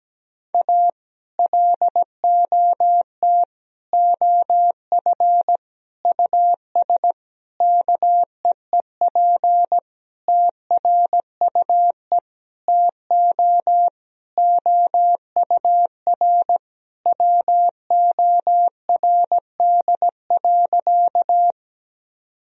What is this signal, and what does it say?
Morse code, heard as a keyed tone, about 17 wpm, 700 Hz. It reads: A LOT OF US KEEP TRUE TO OUR WORD.
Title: Sentences from Top 1000 Words - Review Part 3 - 17wpm